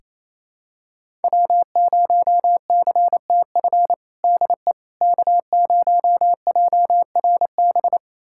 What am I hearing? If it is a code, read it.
Morse code, heard as a keyed tone, about 28 wpm, 700 Hz. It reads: W0CTF DE K0JRB
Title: USA Callsign DE Callsign - 5 Characters - 28wpm